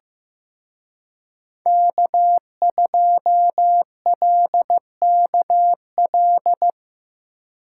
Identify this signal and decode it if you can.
Morse code, heard as a keyed tone, about 17 wpm, 700 Hz. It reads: K2LKL
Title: Speed Racing - USA 1x3 Call Signs 15wpm